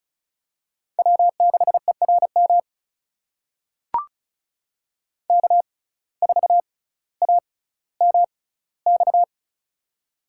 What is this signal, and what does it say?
Morse code, heard as a keyed tone, about 36 wpm, spaced as for 18 wpm, 700 Hz. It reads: W6ERM K4AMX
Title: USA Call Signs - 5 Characters - 6x Character Spacing - 35wpm